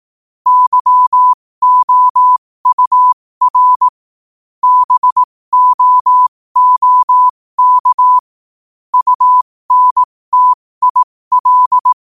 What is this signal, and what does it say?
Morse code, heard as a keyed tone, about 18 wpm, spaced as for 15 wpm, 1000 Hz. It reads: YOUR BOOK UNTIL
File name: qotdfiles / 15wpm